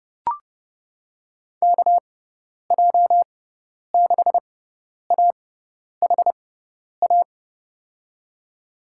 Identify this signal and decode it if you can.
Morse code, heard as a keyed tone, about 30 wpm, spaced as for 10 wpm, 700 Hz. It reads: KJ6AHA
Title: USA Call Signs - 6 Characters - 6x Character Spacing - 30wpm